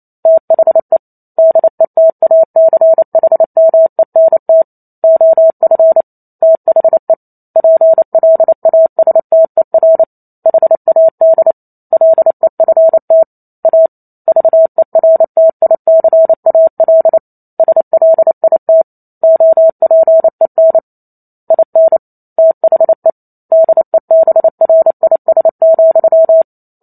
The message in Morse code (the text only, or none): THE DETACHMENT OF THE PLASTER HAD LEFT A VERTICAL SLIT OPEN IN THE DEBRIS,